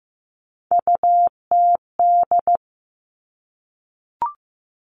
Morse code, heard as a keyed tone, about 15 wpm, 700 Hz. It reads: UTD
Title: Three Challenging Letter-Number Combinations 15wpm